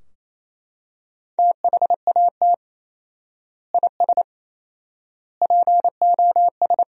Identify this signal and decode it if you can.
Morse code, heard as a keyed tone, about 28 wpm, 700 Hz. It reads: THAT IS POS
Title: 4x Word-Spacing - Sentences from Top 500 Words - 28wpm